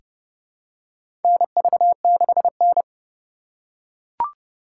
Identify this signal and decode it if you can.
Morse code, heard as a keyed tone, about 30 wpm, 700 Hz. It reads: NV6N
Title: USA Call Signs - 4 Characters - 2x Character Spacing - 30wpm